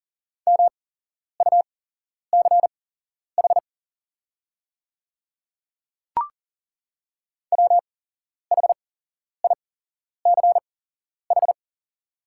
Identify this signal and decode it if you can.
Morse code, heard as a keyed tone, about 40 wpm, spaced as for 11 wpm, 700 Hz. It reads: MUCH WHICH
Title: 8x Character Spacing - Top 200 Words - 40wpm